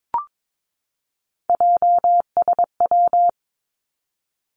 Morse code, heard as a keyed tone, about 22 wpm, 700 Hz. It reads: JSW
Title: Three Challenging Letter-Number Combinations 22wpm